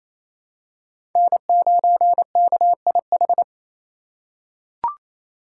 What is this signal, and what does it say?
Morse code, heard as a keyed tone, about 28 wpm, 700 Hz. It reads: N9KIH
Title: USA Call Signs - 4x Character Spacing - 28wpm